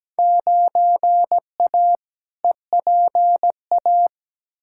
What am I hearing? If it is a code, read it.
Morse code, heard as a keyed tone, about 17 wpm, 700 Hz. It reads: 9A EPA
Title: Field Day Exchanges Encore - 17wpm